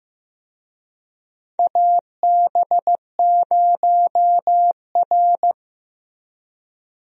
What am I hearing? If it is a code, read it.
Morse code, heard as a keyed tone, about 16 wpm, 700 Hz. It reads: AB0R
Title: USA Call Signs - 4 Characters - Repeated 2x - 15wpm